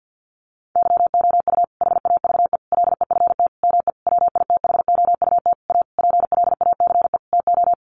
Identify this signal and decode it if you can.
Morse code, heard as a keyed tone, about 49 wpm, 700 Hz. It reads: YOU HAVE LEFT ME WITHOUT A PLACE TO
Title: Sentences from Top 600 Words 50wpm